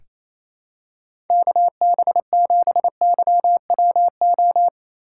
Morse code, heard as a keyed tone, about 28 wpm, 700 Hz. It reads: KB7YWO
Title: USA Call Signs - 6 Characters - 3x Character Spacing - 28wpm